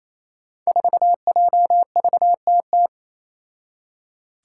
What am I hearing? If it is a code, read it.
Morse code, heard as a keyed tone, about 28 wpm, 700 Hz. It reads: 4JVTT
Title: Five Challenging Letter-Number Combinations 28wpm